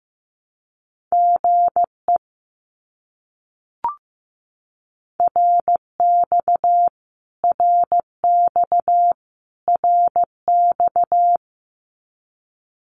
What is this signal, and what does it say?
Morse code, heard as a keyed tone, about 15 wpm, 700 Hz. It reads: GE RX RX RX
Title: Abbreviations Repeated 3x - 15wpm